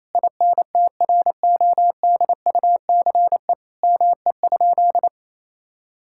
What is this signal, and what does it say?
Morse code, heard as a keyed tone, about 28 wpm, 700 Hz. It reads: INTRODUCE ME?
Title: Sentences from Top 1600 Words 28wpm